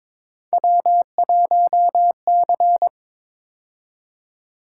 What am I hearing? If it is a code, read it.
Morse code, heard as a keyed tone, about 26 wpm, 700 Hz. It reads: W1C
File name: Speed-Racing - Three Letter-Number Combinations - 22wpm